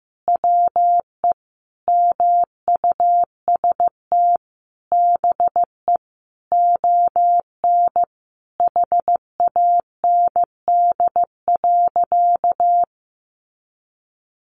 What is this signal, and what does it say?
Morse code, heard as a keyed tone, about 15 wpm, 700 Hz. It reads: WE MUST BE ON HAND.
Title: Sentences from Top 1500 Words 15wpm